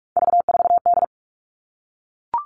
Morse code, heard as a keyed tone, about 44 wpm, 700 Hz. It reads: V4D
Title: All Cut Numbers - 3-Digit 45wpm